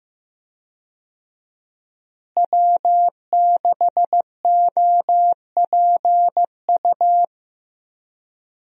Morse code, heard as a keyed tone, about 15 wpm, 700 Hz. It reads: W6OPU
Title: USA Call Signs - 5 Characters - Repeated 3x - 15wpm